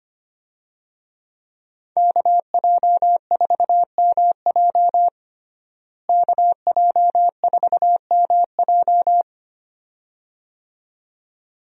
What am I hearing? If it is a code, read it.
Morse code, heard as a keyed tone, about 25 wpm, 700 Hz. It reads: KJ4MJ KJ4MJ
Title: USA Call Signs - 5 Characters - Repeated 2x - 25wpm